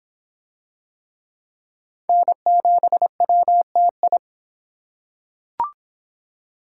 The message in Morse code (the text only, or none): N7WTI